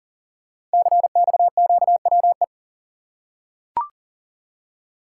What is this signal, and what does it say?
Morse code, heard as a keyed tone, about 39 wpm, 690 Hz. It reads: CXQWE
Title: Code Groups - 5 Letters - 40wpm